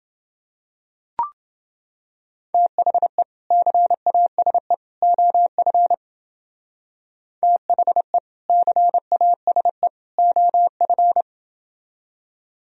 Morse code, heard as a keyed tone, about 28 wpm, 700 Hz. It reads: THE CASE OF THE CASE OF